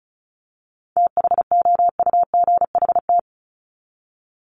Morse code, heard as a keyed tone, about 34 wpm, 700 Hz. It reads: THOUGHT
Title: Top 500 Words 35wpm